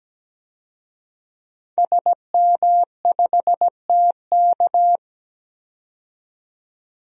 Morse code, heard as a keyed tone, about 17 wpm, 700 Hz. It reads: SM5TK
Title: DXCC-100 Call Signs - 5 Characters - 17wpm